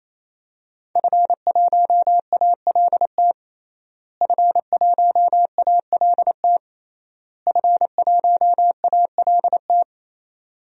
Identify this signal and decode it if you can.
Morse code, heard as a keyed tone, about 28 wpm, 700 Hz. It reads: F1ALT F1ALT F1ALT
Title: DXCC-100 Call Signs - 5 Characters - Repeated 3x - 28wpm